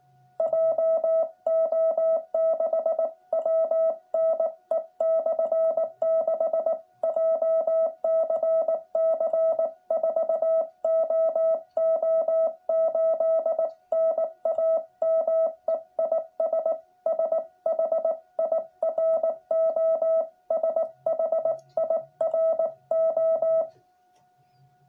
Morse code, heard as a keyed tone, about 18 wpm, 650 Hz. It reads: JO6WNE/6 JCC 4OO8 NAME IS SHIRO SHIRO
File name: jh8cbh / 2021cw